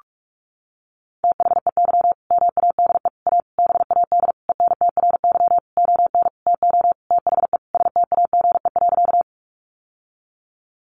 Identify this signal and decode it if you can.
Morse code, heard as a keyed tone, about 44 wpm, 700 Hz. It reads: THEY MADE A BAD ENTRY ON TO THE STAGE.
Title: Sentences from Top 1500 Words 45wpm